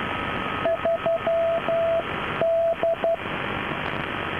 Morse code, heard as a keyed tone, about 12 wpm, 660 Hz, with radio noise. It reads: 3D
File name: 3D- -0055utc-4200khz